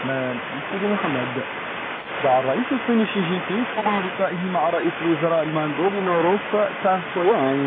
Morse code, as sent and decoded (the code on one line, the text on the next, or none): none